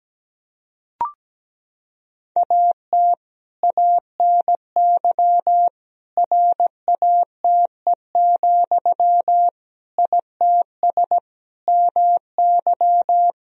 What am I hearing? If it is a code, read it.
Morse code, heard as a keyed tone, about 17 wpm, 700 Hz. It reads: AT ANY RATE, ITS MY
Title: Sentences from Top 1000 Words - Review Part 4 - 17wpm